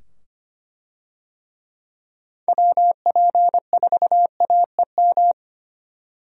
Morse code, heard as a keyed tone, about 25 wpm, 700 Hz. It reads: WP4AEM
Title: USA Call Signs - 6 Characters - 25wpm